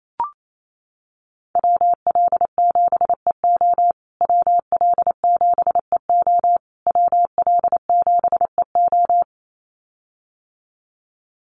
Morse code, heard as a keyed tone, about 28 wpm, 700 Hz. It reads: WL7EO WL7EO WL7EO
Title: USA Call Signs Repeated 3x 28wpm